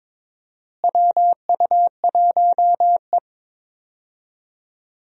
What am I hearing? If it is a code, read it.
Morse code, heard as a keyed tone, about 22 wpm, 700 Hz. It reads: WU1E
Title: USA Call Signs - 4 Characters - Repeated 3x - 22wpm